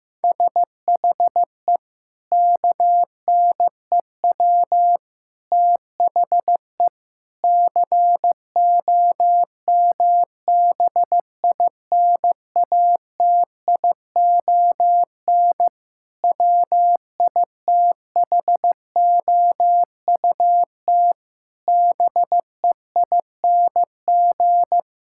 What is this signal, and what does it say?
Morse code, heard as a keyed tone, about 15 wpm, 700 Hz. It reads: SHE KNEW THE COMBINATION WITHOUT BEING
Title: Sentences from Top 1000 Words - Review Part 6 - 15wpm